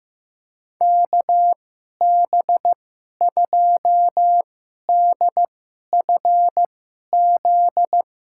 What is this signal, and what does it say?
Morse code, heard as a keyed tone, about 15 wpm, spaced as for 11 wpm, 700 Hz. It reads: KB2DFZ
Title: USA Call Signs - 6 Characters - 2x Character Spacing - 15wpm